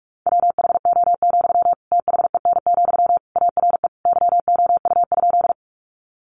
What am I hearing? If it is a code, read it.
Morse code, heard as a keyed tone, about 44 wpm, 700 Hz. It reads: WHO, THEN, ARE YOU?
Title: Sentences from Top 100 Words 45wpm